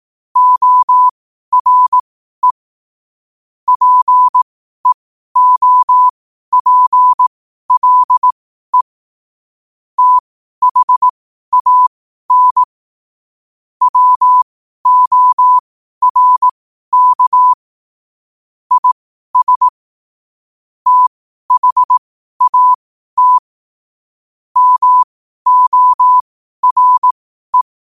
Morse code, heard as a keyed tone, about 18 wpm, spaced as for 12 wpm, 1000 Hz. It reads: ORE PEOPLE THAN WORK IS THAT MORE